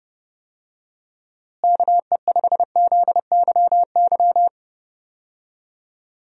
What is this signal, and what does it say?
Morse code, heard as a keyed tone, about 30 wpm, 700 Hz. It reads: KE5ZYY
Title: USA Call Signs - 6 Characters - Repeated 2x - 30wpm